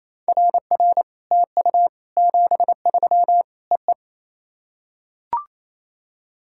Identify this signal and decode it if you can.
Morse code, heard as a keyed tone, about 28 wpm, 700 Hz. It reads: RR TU 73 EE
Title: SOTA QSO Practice 28wpm